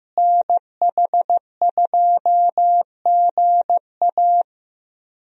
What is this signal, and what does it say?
Morse code, heard as a keyed tone, about 15 wpm, 700 Hz. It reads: NH2GA
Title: USA Call Signs - 5 Characters - Repeated 3x - 15wpm